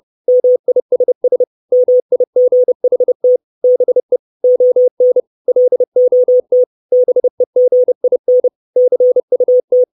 Morse code, heard as a keyed tone, about 30 wpm, 500 Hz. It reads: MISS MIGHT BE ON LOT BEGIN CUT
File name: 632 Words 30WPM